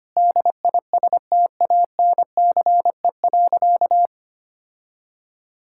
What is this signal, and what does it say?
Morse code, heard as a keyed tone, about 25 wpm, 700 Hz. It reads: DISTANCE.
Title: Sentences from Top 1600 Words 25wpm